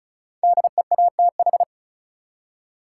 Morse code, heard as a keyed tone, about 36 wpm, 700 Hz. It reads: DEATH
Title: Five Letter Words - 35wpm